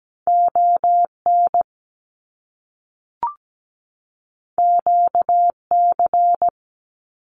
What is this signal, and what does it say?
Morse code, heard as a keyed tone, about 17 wpm, 700 Hz. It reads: ON QC
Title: Canadian Provinces and Territories 17wpm